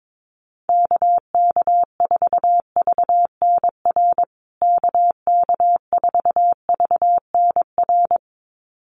Morse code, heard as a keyed tone, about 22 wpm, 700 Hz. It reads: KK4VNR KK4VNR
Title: USA Call Signs Repeated 2x 22wpm